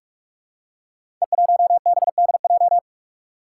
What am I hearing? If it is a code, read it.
Morse code, heard as a keyed tone, about 44 wpm, 700 Hz. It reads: E1BDJ
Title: Code Groups - 5 Letter:Numbers - 45wpm